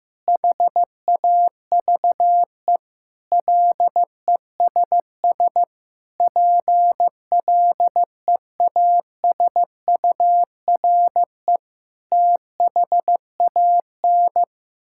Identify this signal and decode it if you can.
Morse code, heard as a keyed tone, about 15 wpm, 700 Hz. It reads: HAVE LESS PLEASURE THAN
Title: Sentences from Top 1600 Words 15wpm